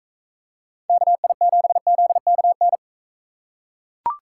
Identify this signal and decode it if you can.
Morse code, heard as a keyed tone, about 42 wpm, 700 Hz. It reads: KI7ZKN